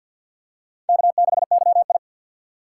Morse code, heard as a keyed tone, about 52 wpm, 700 Hz. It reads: K6YI
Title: USA Call Signs - 4 Characters - 2x Character Spacing - 50wpm